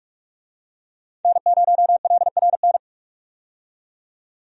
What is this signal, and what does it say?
Morse code, heard as a keyed tone, about 46 wpm, 700 Hz. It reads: N0LRN
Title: USA Call Signs - 5 Characters - 45wpm